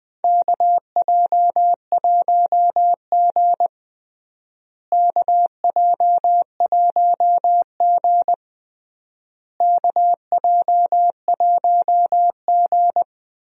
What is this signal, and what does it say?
Morse code, heard as a keyed tone, about 21 wpm, 700 Hz. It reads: KJ1G KJ1G KJ1G
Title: USA Call Signs - 4 Characters - Repeated 3x - 20wpm